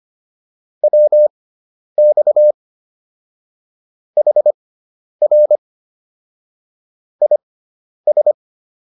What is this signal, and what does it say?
Morse code, heard as a keyed tone, about 25 wpm, spaced as for 10 wpm, 600 Hz. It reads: WX HR IS